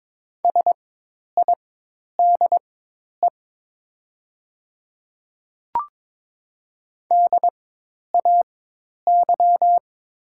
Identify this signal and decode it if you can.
Morse code, heard as a keyed tone, about 22 wpm, spaced as for 10 wpm, 700 Hz. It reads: SIDE DAY
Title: 4x Character Spacing - Top 300 Words - 22wpm